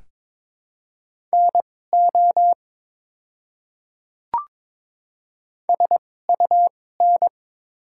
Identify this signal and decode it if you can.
Morse code, heard as a keyed tone, about 22 wpm, spaced as for 16 wpm, 700 Hz. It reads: NO SUN
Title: 2x Character Spacing - Top 300 Words - 22wpm